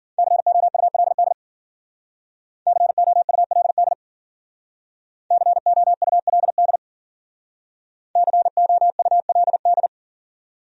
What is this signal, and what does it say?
Morse code, heard as a keyed tone, about 50 wpm, 700 Hz. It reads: COULD COULD COULD COULD
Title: Speed-Racing Top 200 Words 40wpm